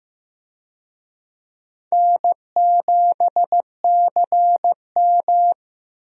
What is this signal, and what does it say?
Morse code, heard as a keyed tone, about 15 wpm, 700 Hz. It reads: N7CM
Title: USA Call Signs - 4 Characters - 15wpm